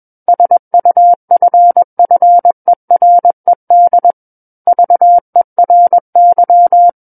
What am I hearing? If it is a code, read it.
Morse code, heard as a keyed tone, about 21 wpm, 700 Hz. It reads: SUFFERED VERY
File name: Book One / 12-Chapter 12 - 21 WPM